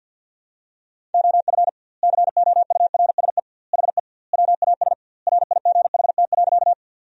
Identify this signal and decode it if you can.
Morse code, heard as a keyed tone, about 52 wpm, 700 Hz. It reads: OF COURSE HE WAS RIGHT.
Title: 2x Word-Spacing - Sentences from Top 500 Words - 50wpm